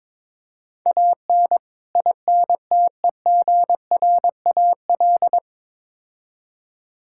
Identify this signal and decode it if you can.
Morse code, heard as a keyed tone, about 22 wpm, 700 Hz. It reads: AN INTEGRAL
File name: practice / 22wpm / Build A Phrase - 5-Word Phrases - 22wpm